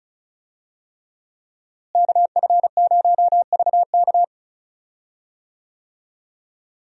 Morse code, heard as a keyed tone, about 36 wpm, 700 Hz. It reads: KF0VK
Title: USA Call Signs - 5 Characters - Repeated 3x - 35wpm